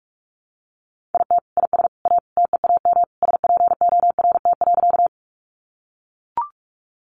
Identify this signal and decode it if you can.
Morse code, heard as a keyed tone, about 44 wpm, 700 Hz. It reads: IT IS A TEAM SPORT.
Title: Sentences from Top 1500 Words 45wpm